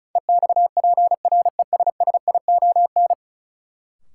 Morse code, heard as a keyed tone, about 34 wpm, 700 Hz. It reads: EXPRESSION
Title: Sentences from Top 1200 Words - New Word Drill - 35wpm